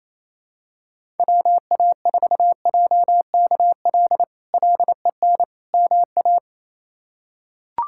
Morse code, heard as a keyed tone, about 27 wpm, 700 Hz. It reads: WA4JKL LEN MA
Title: K1USN SST QSO Practice - 28wpm